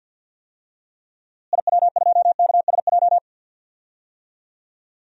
Morse code, heard as a keyed tone, about 52 wpm, 700 Hz. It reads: IW2KSJ
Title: DXCC-100 Call Signs - 6 Characters - Repeated 2x - 50wpm